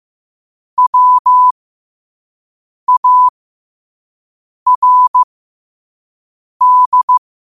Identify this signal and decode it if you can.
Morse code, heard as a keyed tone, about 15 wpm, spaced as for 5 wpm, 1000 Hz. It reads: WARD
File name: qotdfiles / 5wpm